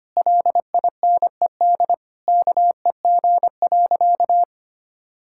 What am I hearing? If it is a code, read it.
Morse code, heard as a keyed tone, about 25 wpm, 700 Hz. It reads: LINED KEG.
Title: Pangram Challenge v2 - 25wpm